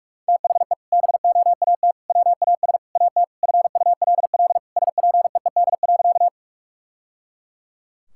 Morse code, heard as a keyed tone, about 46 wpm, 700 Hz. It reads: THE BOAT WAS AT FULL SPEED.
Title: Sentences from Top 1000 Words - Review Part 5 - 45wpm